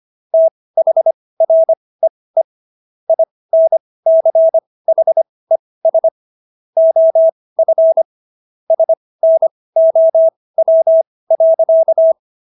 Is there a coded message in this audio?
Yes: Morse code from 0.3 to 12.1 s, heard as a keyed tone, about 25 wpm, spaced as for 18 wpm, 650 Hz. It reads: THREE INCHES OF SNOW.